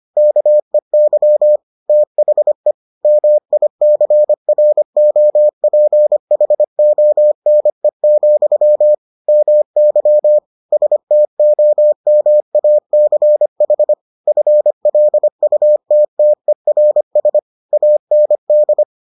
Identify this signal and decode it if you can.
Morse code, heard as a keyed tone, about 25 wpm, 600 Hz. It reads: KEY THE MICROPHONE, MY STOMACH FLUTTERS AND